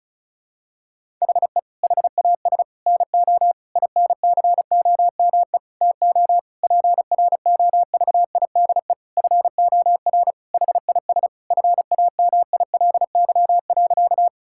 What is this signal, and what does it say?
Morse code, heard as a keyed tone, about 36 wpm, 700 Hz. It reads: HE HAS NO INCOME TO PROVIDE FOR HIS FAMILY.
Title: Sentences from Top 1000 Words - Review Part 5 - 35wpm